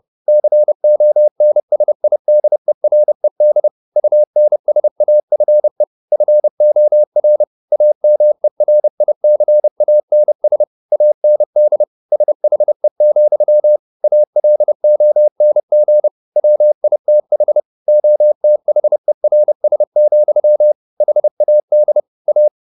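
Morse code, heard as a keyed tone, about 30 wpm, 600 Hz. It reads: CONSIDERED UNSAFE FOR AMERICANS AND SHE, ALONG WITH OTHERS, HAD A
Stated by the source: Morse code at 29 wpm, sending Ken Follett's novel Never in English, from CW med Gnister.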